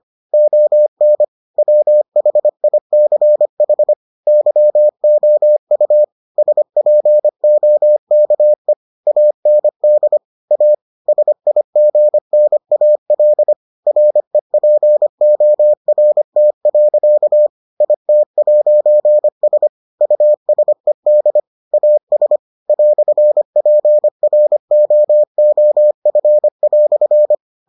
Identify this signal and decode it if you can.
Morse code, heard as a keyed tone, about 25 wpm, 600 Hz. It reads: ON WHICH YOU SPOKE AND A SIGNAL REPORT. IT'S USED AS "PROOF"